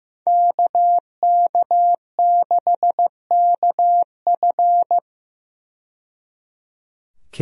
Morse code, heard as a keyed tone, about 15 wpm, 700 Hz. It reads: KK6KF
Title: USA Call Signs - 5 Characters - 15wpm